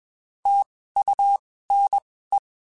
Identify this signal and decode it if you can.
Morse code, heard as a keyed tone, about 21 wpm, spaced as for 15 wpm, 800 Hz. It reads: TUNE
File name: (spaced) / QSOs Spaced-mp3s / WD0FBT @20 sp